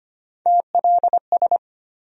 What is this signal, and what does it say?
Morse code, heard as a keyed tone, about 25 wpm, 700 Hz. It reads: TLS
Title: USA Call Sign Postfixes - 3 Letters 25wpm